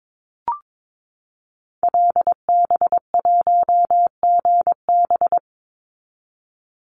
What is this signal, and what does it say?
Morse code, heard as a keyed tone, about 22 wpm, 700 Hz. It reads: LB1GB
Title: Call Signs Repeated 3x - 22wpm